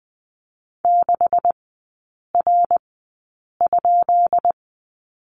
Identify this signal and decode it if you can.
Morse code, heard as a keyed tone, about 20 wpm, spaced as for 8 wpm, 700 Hz. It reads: 6R?
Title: 133 Lesson 22e - Question Mark - Three Characters 20wpm